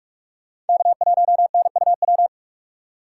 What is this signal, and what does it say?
Morse code, heard as a keyed tone, about 58 wpm, 700 Hz. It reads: K1NUW